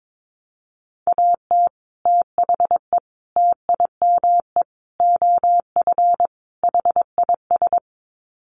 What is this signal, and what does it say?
Morse code, heard as a keyed tone, about 22 wpm, 700 Hz. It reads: AT THE TIME OF HIS
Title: Sets of 5 Words Encore2 22wpm